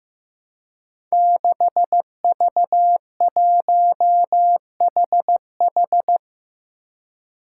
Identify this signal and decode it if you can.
Morse code, heard as a keyed tone, about 15 wpm, 700 Hz. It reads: 6V1HH
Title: Five Challenging Letter-Number Combinations 15wpm